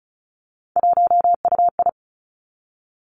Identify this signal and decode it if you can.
Morse code, heard as a keyed tone, about 36 wpm, 700 Hz. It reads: 1UI
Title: Three Challenging Letter-Number Combinations 35wpm